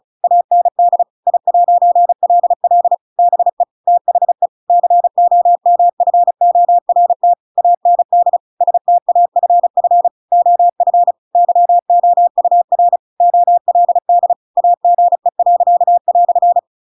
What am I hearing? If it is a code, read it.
Morse code, heard as a keyed tone, about 34 wpm, 700 Hz. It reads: AND I'LL BE THE COMFORT AND STAFF OF YOUR OLD AGE."
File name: Pinocchio / Pinocchio ch8 35 35 ik1hge